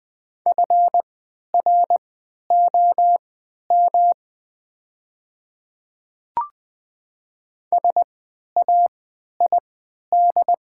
Morse code, heard as a keyed tone, about 20 wpm, spaced as for 11 wpm, 700 Hz. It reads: FROM SAID
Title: 3x Character Spacing - Top 200 Words - 20wpm